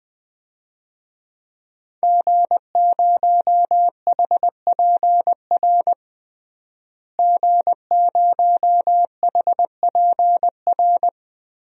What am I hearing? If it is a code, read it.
Morse code, heard as a keyed tone, about 21 wpm, 700 Hz. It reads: G0HPR G0HPR
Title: DXCC-100 Call Signs - 5 Characters - Repeated 3x - 20wpm